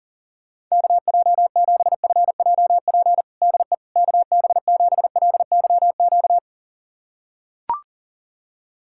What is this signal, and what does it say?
Morse code, heard as a keyed tone, about 40 wpm, 700 Hz. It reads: KJ7FJP DE KB7LYQ
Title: USA Callsign DE Callsign - 6 Characters - 40wpm